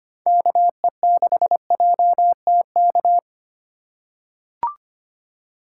Morse code, heard as a keyed tone, about 25 wpm, 700 Hz. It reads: KE6JTK
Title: USA Call Signs - 2x Character Spacing - 25wpm